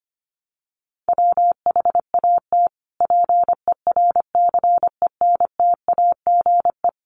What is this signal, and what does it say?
Morse code, heard as a keyed tone, about 25 wpm, 700 Hz. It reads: WHAT PERCENTAGE
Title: Sentences from Top 1300 Words 25wpm